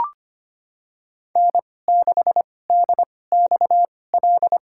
Morse code, heard as a keyed tone, about 25 wpm, spaced as for 18 wpm, 700 Hz. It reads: N6DXL
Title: USA Call Signs - 5 Characters - 2x Character Spacing - 25wpm